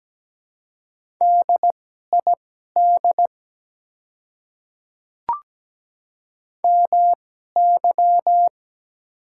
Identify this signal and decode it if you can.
Morse code, heard as a keyed tone, about 17 wpm, spaced as for 12 wpm, 690 Hz. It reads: DID MY
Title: 2x Character Spacing - Top 100 Words - 17wpm